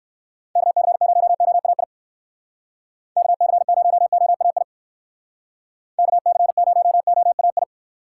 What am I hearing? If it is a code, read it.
Morse code, heard as a keyed tone, about 55 wpm, 700 Hz. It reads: KC0OAI KC0OAI KC0OAI
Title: Speed Racing - USA 2x3 Call Signs 45wpm